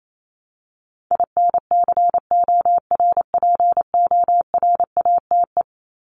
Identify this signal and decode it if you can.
Morse code, heard as a keyed tone, about 28 wpm, 700 Hz. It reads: INCORPORATE
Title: Top 1751-2000 Words 28wpm